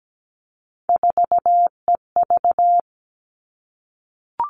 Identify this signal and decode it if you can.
Morse code, heard as a keyed tone, about 17 wpm, 700 Hz. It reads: 4EV
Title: All Cut Numbers - 3-Digit 17wpm